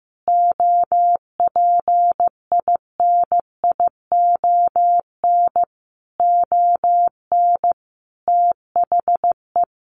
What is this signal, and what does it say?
Morse code, heard as a keyed tone, about 15 wpm, 700 Hz. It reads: OPINION ON THE
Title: Sentences from Top 1200 Words 15wpm